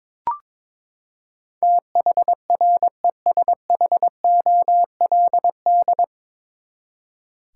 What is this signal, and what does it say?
Morse code, heard as a keyed tone, about 22 wpm, 700 Hz. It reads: THRESHOLD
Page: Morse Code Ninja practice 22wpm